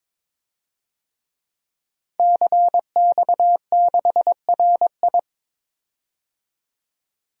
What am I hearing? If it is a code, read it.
Morse code, heard as a keyed tone, about 21 wpm, 700 Hz. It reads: CX6RI
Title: DXCC-100 Call Signs - 5 Characters - 22wpm